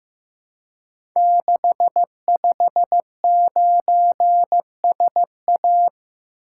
Morse code, heard as a keyed tone, about 15 wpm, 700 Hz. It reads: 659SA
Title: Code Groups - 5 Letter:Numbers - 15wpm